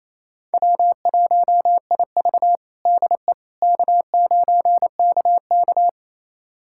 Morse code, heard as a keyed tone, about 28 wpm, 700 Hz. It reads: W1IV DE K9KK
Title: USA Callsign DE Callsign - 4 Characters - 28wpm